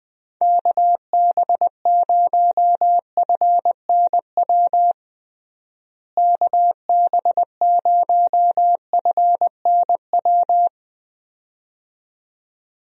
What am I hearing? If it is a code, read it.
Morse code, heard as a keyed tone, about 21 wpm, 700 Hz. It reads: KB0FNW KB0FNW
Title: USA Call Signs - 6 Characters - Repeated 2x - 20wpm